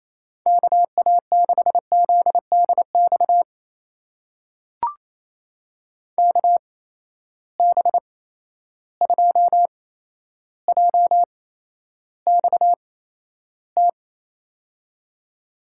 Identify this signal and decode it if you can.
Morse code, heard as a keyed tone, about 28 wpm, spaced as for 12 wpm, 700 Hz. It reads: KA6ZDX KB2JXT
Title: USA Call Signs - 8x Character Spacing - 28wpm